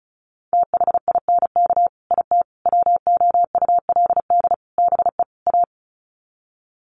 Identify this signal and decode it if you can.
Morse code, heard as a keyed tone, about 34 wpm, 700 Hz. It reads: THINK IT WOULD BE A
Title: Sets of 5 Words Encore2 35wpm